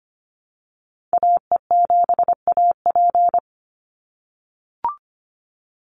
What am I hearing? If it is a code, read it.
Morse code, heard as a keyed tone, about 25 wpm, 700 Hz. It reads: AE7AP
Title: NA SOTA Chaser Callsigns 25wpm